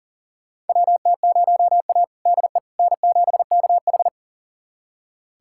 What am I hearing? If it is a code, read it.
Morse code, heard as a keyed tone, about 40 wpm, 700 Hz. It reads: WT0A DE N7KH
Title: USA Callsign DE Callsign - 4 Characters - 40wpm